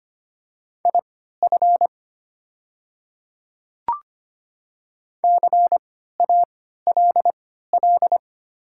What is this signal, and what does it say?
Morse code, heard as a keyed tone, about 25 wpm, spaced as for 14 wpm, 700 Hz. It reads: IF CALL
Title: 3x Character Spacing - Top 100 Words - 25wpm